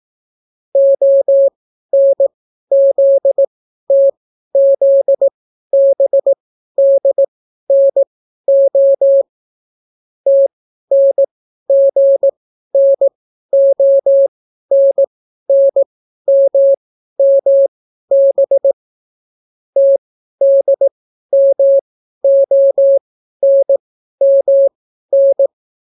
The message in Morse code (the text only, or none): ONZTZBDNO TNGNONNMMB TDMONMN